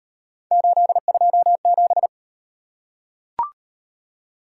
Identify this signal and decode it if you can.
Morse code, heard as a keyed tone, about 38 wpm, 700 Hz. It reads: 827